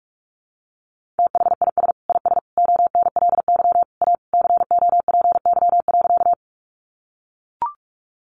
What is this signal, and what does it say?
Morse code, heard as a keyed tone, about 44 wpm, 700 Hz. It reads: THIS IS ONLY A COPY.
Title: Sentences from Top 1400 Words 45wpm